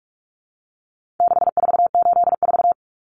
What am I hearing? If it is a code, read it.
Morse code, heard as a keyed tone, about 44 wpm, 700 Hz. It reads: 6484
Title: Numbers - 4-Digit 45wpm